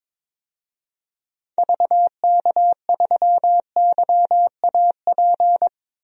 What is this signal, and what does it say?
Morse code, heard as a keyed tone, about 21 wpm, 700 Hz. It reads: VK3YAP
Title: DXCC-100 Call Signs - 6 Characters - 22wpm